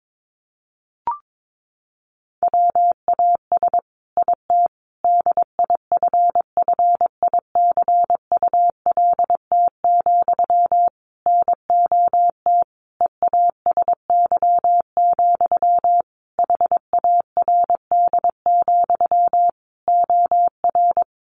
Morse code, heard as a keyed tone, about 22 wpm, 700 Hz. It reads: WAS IT DIFFICULT, NOT EASY, HARD, OR
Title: Sentences from Top 1200 Words 22wpm